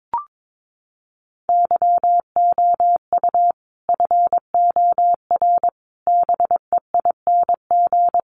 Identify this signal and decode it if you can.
Morse code, heard as a keyed tone, about 22 wpm, 700 Hz. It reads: YOU FOR BEING
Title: Sets of 5 Words 22wpm